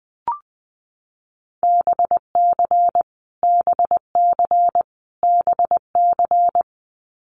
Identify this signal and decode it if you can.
Morse code, heard as a keyed tone, about 20 wpm, 700 Hz. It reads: BC BC BC
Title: Canadian Provinces and Territories - Repeated 3x - 20wpm